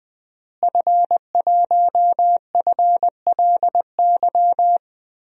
Morse code, heard as a keyed tone, about 20 wpm, 700 Hz. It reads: F1FLY
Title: DXCC-100 Call Signs - 5 Characters - Repeated 3x - 20wpm